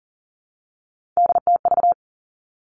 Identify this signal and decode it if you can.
Morse code, heard as a keyed tone, about 41 wpm, 700 Hz. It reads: DTV
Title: Three Challenging Letter-Number Combinations 40wpm